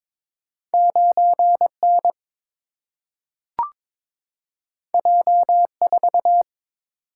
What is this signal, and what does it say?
Morse code, heard as a keyed tone, about 22 wpm, 700 Hz. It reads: 9N J4
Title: Single-Letter Single-Number 22wpm